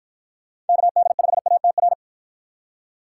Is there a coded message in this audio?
Yes: Morse code from 0.7 to 1.9 s, heard as a keyed tone, about 50 wpm, 700 Hz. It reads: KD5ATR